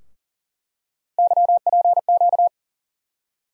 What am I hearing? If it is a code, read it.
Morse code, heard as a keyed tone, about 39 wpm, 700 Hz. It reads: YPQ
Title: USA Call Sign Postfixes - 3 Letters 40wpm